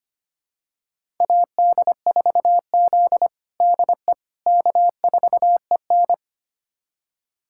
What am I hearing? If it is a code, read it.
Morse code, heard as a keyed tone, about 25 wpm, 700 Hz. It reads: AD4Z DE K4EN
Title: USA Callsign DE Callsign - 4 Characters - 25wpm